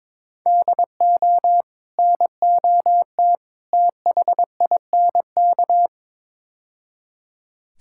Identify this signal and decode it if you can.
Morse code, heard as a keyed tone, about 22 wpm, 700 Hz. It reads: DO NOT THINK
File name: practice / 22wpm / Build A Phrase - 5-Word Phrases - 22wpm